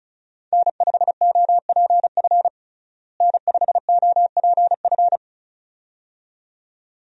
Morse code, heard as a keyed tone, about 34 wpm, 700 Hz. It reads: N5OPF N5OPF
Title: USA Call Signs - 5 Characters - Repeated 2x - 35wpm